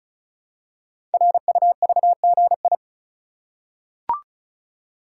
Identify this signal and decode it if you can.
Morse code, heard as a keyed tone, about 35 wpm, 690 Hz. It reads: RUVGI
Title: Code Groups - 5 Letter:Numbers - 35wpm